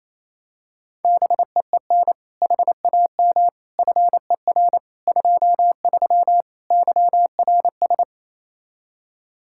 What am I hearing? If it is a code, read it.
Morse code, heard as a keyed tone, about 28 wpm, 700 Hz. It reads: BEEN HAM FER 23 YRS
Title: Practice QSO - Exchange 3 - More Info - 28wpm